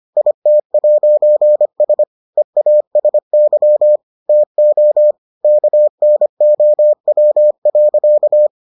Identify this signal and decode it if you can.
Morse code, heard as a keyed tone, about 25 wpm, 600 Hz. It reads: IT'S EASY TO KNOW.